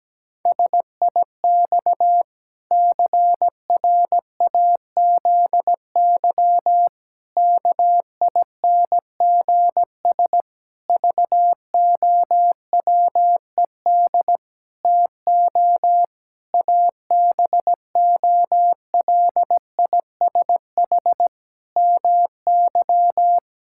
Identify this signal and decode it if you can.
Morse code, heard as a keyed tone, about 17 wpm, 700 Hz. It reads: SIX CRAZY KINGS VOWED TO ABOLISH MY
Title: Pangram Challenge v2 - 17wpm